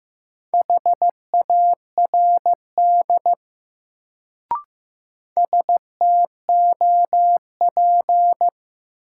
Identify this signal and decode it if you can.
Morse code, heard as a keyed tone, about 15 wpm, 700 Hz. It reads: HARD STOP
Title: Sentences from Top 300 Words - New Word Drill - 15wpm